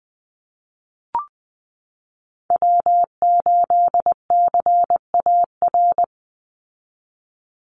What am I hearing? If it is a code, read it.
Morse code, heard as a keyed tone, about 20 wpm, 700 Hz. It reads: W8CAR
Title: Call Signs Repeated 3x - 20wpm